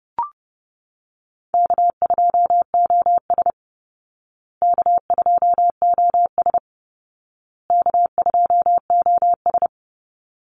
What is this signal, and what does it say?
Morse code, heard as a keyed tone, about 30 wpm, 700 Hz. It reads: K2OS K2OS K2OS
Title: Call Signs Repeated 2x - 30wpm